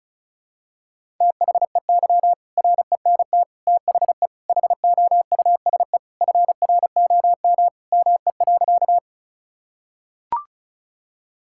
Morse code, heard as a keyed tone, about 36 wpm, 700 Hz. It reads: THEY RENT THE HOUSE FROM ME.
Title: Sentences from Top 1000 Words - Review Part 5 - 35wpm